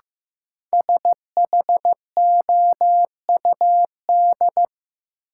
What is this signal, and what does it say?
Morse code, heard as a keyed tone, about 15 wpm, 700 Hz. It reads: SHOUD
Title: Sentences from Top 1000 Words - Review Part 4 - New Word Drill - 15wpm